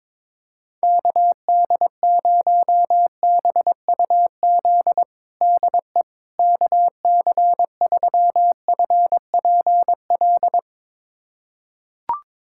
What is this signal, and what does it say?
Morse code, heard as a keyed tone, about 22 wpm, 700 Hz. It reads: KD0BUZ DE KC3FPL
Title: USA Callsign DE Callsign - 6 Characters - 22wpm